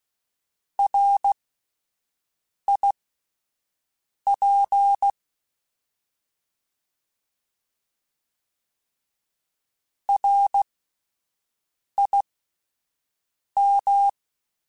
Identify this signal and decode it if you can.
Morse code, heard as a keyed tone, about 16 wpm, spaced as for 5 wpm, 800 Hz. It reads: RIP RIM